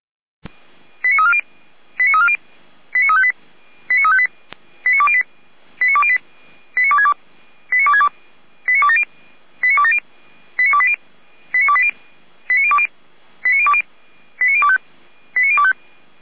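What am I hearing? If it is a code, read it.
Five-tone selective call (ZVEI), heard as quick runs of five tones, tones of about 70 ms. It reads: 89340 89340 89358 89358 89208 89262 89262 89270 89270 89280 89280 89020 89020 89025 89025
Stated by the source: ZVEI five-tone selective call 89333 received by SDR.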